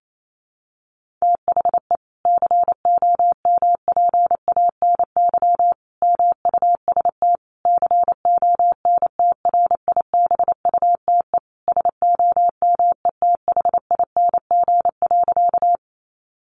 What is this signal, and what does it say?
Morse code, heard as a keyed tone, about 28 wpm, 700 Hz. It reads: THE COMPANY MUST CONTRIBUTE SOMETHING.
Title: Sentences from Top 1400 Words 28wpm